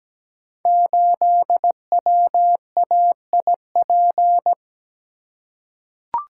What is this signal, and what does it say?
Morse code, heard as a keyed tone, about 17 wpm, 700 Hz. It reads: 8WAIP
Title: Code Groups - 5 Letter:Numbers - 17wpm